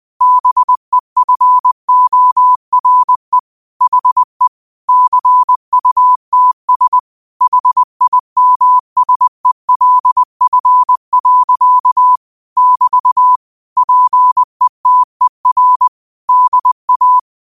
Morse code, heard as a keyed tone, about 20 wpm, 1000 Hz. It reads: BEFORE HE CUTS HIMSELF. <BT> PETER DA
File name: qotdfiles / 20wpm